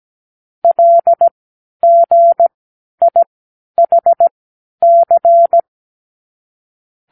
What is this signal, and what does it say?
Morse code, heard as a keyed tone, about 17 wpm, spaced as for 10 wpm, 680 Hz. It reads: LGIHC
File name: k7qo / Lesson18(A-L)